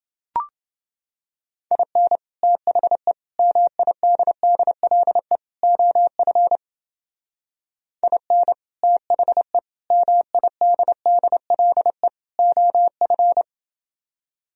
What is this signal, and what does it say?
Morse code, heard as a keyed tone, about 31 wpm, 700 Hz. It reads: IN THE MIDDLE OF IN THE MIDDLE OF